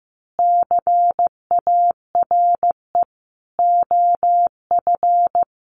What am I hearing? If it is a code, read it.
Morse code, heard as a keyed tone, about 15 wpm, 700 Hz. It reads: CARE OF
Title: Sentences from Top 1200 Words 15wpm